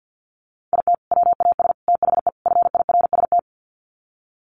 Morse code, heard as a keyed tone, about 48 wpm, 700 Hz. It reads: IT WAS THE FIRST
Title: Sets of 4 Words 50wpm